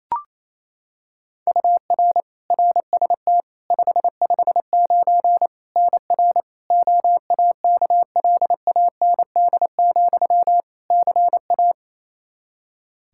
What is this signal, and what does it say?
Morse code, heard as a keyed tone, about 28 wpm, 700 Hz. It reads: UR RST 559 NR OAKLAND, CA
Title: K1USN SST QSO Practice 28wpm